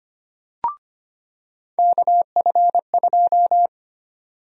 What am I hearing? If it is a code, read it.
Morse code, heard as a keyed tone, about 25 wpm, 700 Hz. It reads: KF2
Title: USA Call Sign Prefixes - 25wpm